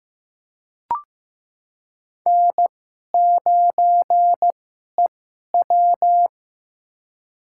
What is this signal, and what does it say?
Morse code, heard as a keyed tone, about 16 wpm, spaced as for 11 wpm, 700 Hz. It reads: N9EW
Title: USA Call Signs - 2x Character Spacing - 15wpm